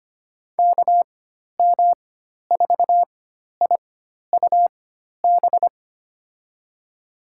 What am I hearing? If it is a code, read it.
Morse code, heard as a keyed tone, about 25 wpm, spaced as for 12 wpm, 700 Hz. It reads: KM4IUB
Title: USA Call Signs - 6 Characters - 4x Character Spacing - 25wpm